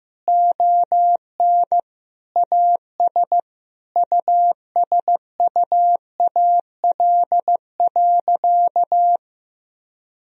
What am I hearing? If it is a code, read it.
Morse code, heard as a keyed tone, about 15 wpm, 700 Hz. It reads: ON AS USUAL.
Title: Sentences from Top 1000 Words - Review Part 2 - 15wpm